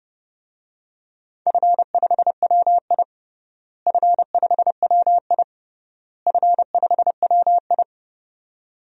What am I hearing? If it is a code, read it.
Morse code, heard as a keyed tone, about 30 wpm, 700 Hz. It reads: F5WI F5WI F5WI
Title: DXCC-100 Call Signs - 4 Characters - Repeated 3x - 30wpm